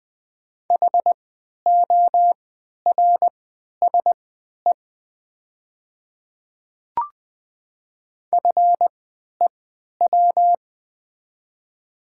Morse code, heard as a keyed tone, about 21 wpm, spaced as for 11 wpm, 700 Hz. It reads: HORSE FEW
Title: 3x Character Spacing - Top 500 Words - 20wpm